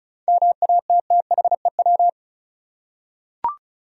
Morse code, heard as a keyed tone, about 36 wpm, 700 Hz. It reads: MATTHEW
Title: USA Common Names - 35wpm